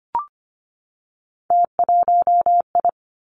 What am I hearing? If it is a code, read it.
Morse code, heard as a keyed tone, about 25 wpm, 700 Hz. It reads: T1I
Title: Three Challenging Letter-Number Combinations 25wpm